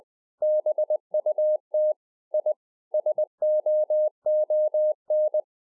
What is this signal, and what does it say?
Morse code, heard as a keyed tone, about 20 wpm, 610 Hz. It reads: BUT I SOON